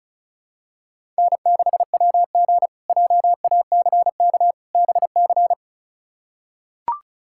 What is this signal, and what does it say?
Morse code, heard as a keyed tone, about 36 wpm, 700 Hz. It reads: N6WG JACK BC
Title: K1USN SST QSO Practice - 35wpm